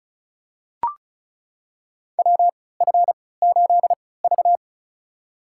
Morse code, heard as a keyed tone, about 34 wpm, spaced as for 20 wpm, 700 Hz. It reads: WF8V
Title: USA Call Signs - 4 Characters - 3x Character Spacing - 35wpm